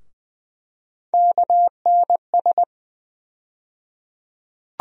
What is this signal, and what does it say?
Morse code, heard as a keyed tone, about 20 wpm, 700 Hz. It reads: KNS
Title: USA Call Sign Postfixes - 3 Letters 20wpm